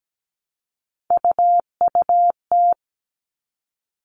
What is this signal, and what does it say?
Morse code, heard as a keyed tone, about 17 wpm, 700 Hz. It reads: UUT
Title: All Cut Numbers - 3-Digit 17wpm